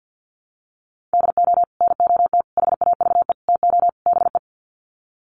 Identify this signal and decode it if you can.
Morse code, heard as a keyed tone, about 50 wpm, 700 Hz. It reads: DO NOT HAVE TO BE
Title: Sets of 5 Words 50wpm